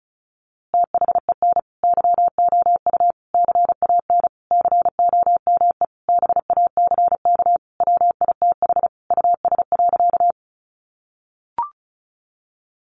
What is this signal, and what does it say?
Morse code, heard as a keyed tone, about 34 wpm, 700 Hz. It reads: THEN YOU CAN COME BACK WITH US.
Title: A Sound Of Thunder 35wpm